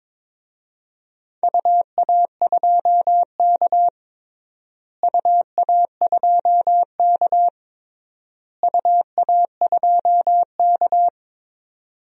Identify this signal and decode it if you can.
Morse code, heard as a keyed tone, about 22 wpm, 700 Hz. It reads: UA2K UA2K UA2K
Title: DXCC-100 Call Signs - 4 Characters - Repeated 3x - 22wpm